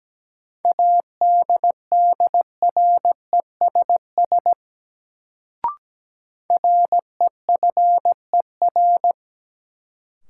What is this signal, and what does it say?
Morse code, heard as a keyed tone, about 17 wpm, 700 Hz. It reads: ADDRESS REFER
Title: Sentences from Top 1000 Words - New Word Drill - 17wpm